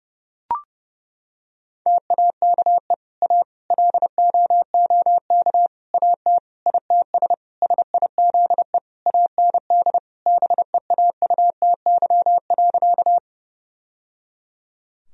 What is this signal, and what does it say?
Morse code, heard as a keyed tone, about 30 wpm, 700 Hz. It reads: TAKE A LOOK AT ITS SIZE AND BEAUTY.
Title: Sentences from Top 1000 Words - Review Part 2 - 30wpm